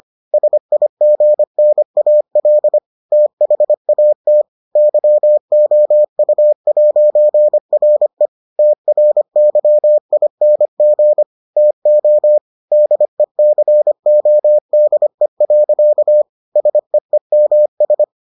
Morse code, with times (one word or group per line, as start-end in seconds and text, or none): 0.3-2.8 SIGNAL
3.1-4.4 THAT
4.8-8.3 YOU'RE
8.6-11.2 TRYING
11.6-12.4 TO
12.7-16.2 DECODE.
16.6-18.0 SEEMS